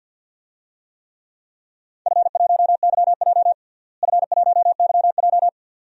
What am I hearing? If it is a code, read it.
Morse code, heard as a keyed tone, about 53 wpm, 700 Hz. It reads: F1YJ F1YJ
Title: DXCC-100 Call Signs - 4 Characters - Repeated 2x - 50wpm